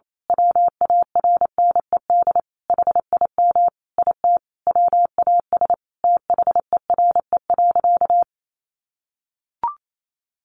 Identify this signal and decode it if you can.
Morse code, heard as a keyed tone, about 28 wpm, 700 Hz. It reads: WARNED HIM IT WAS THERE.
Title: A Sound Of Thunder 28wpm